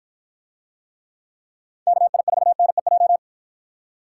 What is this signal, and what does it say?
Morse code, heard as a keyed tone, about 53 wpm, 700 Hz. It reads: KI4NEJ